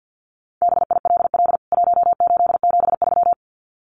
Morse code, heard as a keyed tone, about 50 wpm, 700 Hz. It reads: BILL 1873
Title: CWT Exchange 50wpm